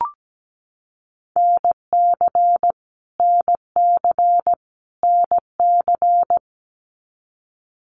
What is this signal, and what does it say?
Morse code, heard as a keyed tone, about 17 wpm, 700 Hz. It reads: NC NC NC
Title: US States Repeated 3x 17wpm